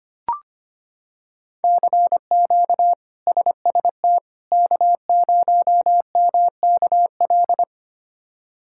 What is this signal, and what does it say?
Morse code, heard as a keyed tone, about 25 wpm, 700 Hz. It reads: CQ SST K0MKL
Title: K1USN SST QSO Practice - 25wpm